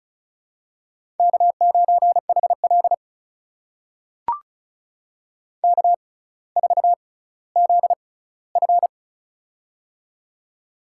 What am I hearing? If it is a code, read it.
Morse code, heard as a keyed tone, about 36 wpm, spaced as for 18 wpm, 700 Hz. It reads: K9HL K4ZF
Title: USA Call Signs - 4 Characters - 6x Character Spacing - 35wpm